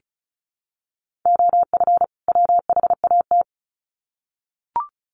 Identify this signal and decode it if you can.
Morse code, heard as a keyed tone, about 34 wpm, 700 Hz. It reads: OF WHAT
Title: Sets of 2 Words 35wpm